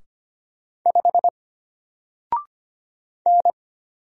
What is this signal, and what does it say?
Morse code, heard as a keyed tone, about 25 wpm, 700 Hz. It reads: 5 N
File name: Single-Letter-Number-25wpm